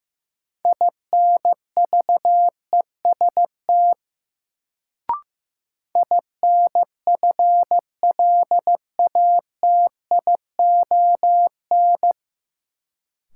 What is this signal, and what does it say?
Morse code, heard as a keyed tone, about 15 wpm, 700 Hz. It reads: INVEST INFLATION
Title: Sentences from Top 1100 Words - New Word Drill Alt - 15wpm